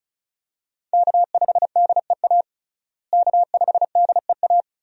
Morse code, heard as a keyed tone, about 36 wpm, 700 Hz. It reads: K5DEA K5DEA
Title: USA Call Signs - 5 Characters - Repeated 2x - 35wpm